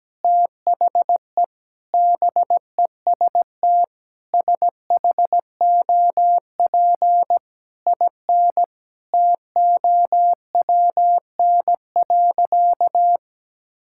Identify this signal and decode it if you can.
Morse code, heard as a keyed tone, about 17 wpm, 700 Hz. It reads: THE BEST SHOP IN TOWN.
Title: Sentences from Top 1600 Words 17wpm